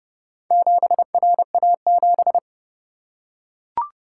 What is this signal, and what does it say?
Morse code, heard as a keyed tone, about 29 wpm, 700 Hz. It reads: 7RA7